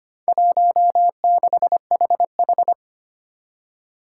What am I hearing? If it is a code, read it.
Morse code, heard as a keyed tone, about 25 wpm, 700 Hz. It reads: 16HH
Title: Four Challenging Letter-Number Combinations 25wpm